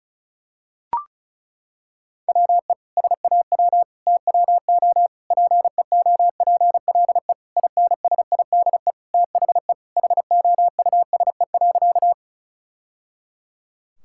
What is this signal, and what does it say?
Morse code, heard as a keyed tone, about 34 wpm, 700 Hz. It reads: WE SAW TWO PEOPLE INSIDE THE HOUSE.
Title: Sentences from Top 1000 Words - Review Part 6 - 35wpm